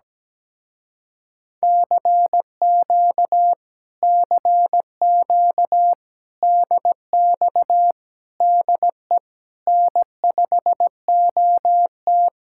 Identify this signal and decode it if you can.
Morse code, heard as a keyed tone, about 17 wpm, 700 Hz. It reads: CQ CQ DX DE N5OT
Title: Speed-Racing Common QSO Phrases Encore - 17wpm